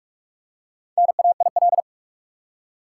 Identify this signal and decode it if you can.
Morse code, heard as a keyed tone, about 48 wpm, 700 Hz. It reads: NAIL
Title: Four Letter Words - 45wpm